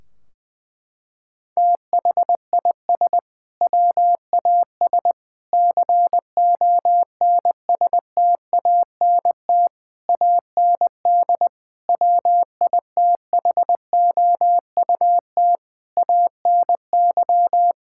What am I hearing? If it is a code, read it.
Morse code, heard as a keyed tone, about 20 wpm, 700 Hz. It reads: THIS WAS CONSTANT AND WITHOUT ANY
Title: Sentences from Top 1600 Words 20wpm